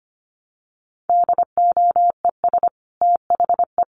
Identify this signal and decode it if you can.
Morse code, heard as a keyed tone, about 25 wpm, 700 Hz. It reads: DOES THE
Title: Sentences from Top 1500 Words 25wpm